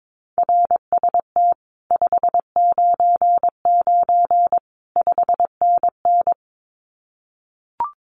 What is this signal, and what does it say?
Morse code, heard as a keyed tone, about 22 wpm, 700 Hz. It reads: RST 599 5NN
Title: Common QSO Phrases Encore 22wpm